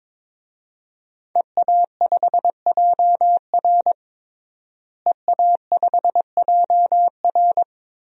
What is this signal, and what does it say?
Morse code, heard as a keyed tone, about 22 wpm, 700 Hz. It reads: EA5JR EA5JR
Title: DXCC-100 Call Signs - 5 Characters - Repeated 2x - 22wpm